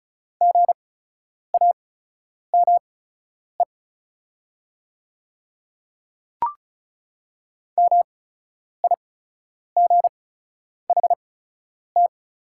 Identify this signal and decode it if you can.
Morse code, heard as a keyed tone, about 36 wpm, spaced as for 10 wpm, 700 Hz. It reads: GAME MIGHT
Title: 8x Character Spacing - Top 500 Words - 35wpm